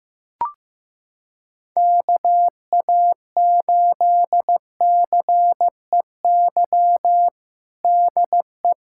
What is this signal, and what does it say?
Morse code, heard as a keyed tone, about 15 wpm, 700 Hz. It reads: KA8CEY DE
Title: USA Callsign DE Callsign - 6 Characters - 15wpm